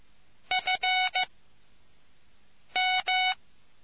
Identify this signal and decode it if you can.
Morse code, heard as a keyed tone, about 15 wpm, spaced as for 5 wpm, 740 Hz. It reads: FM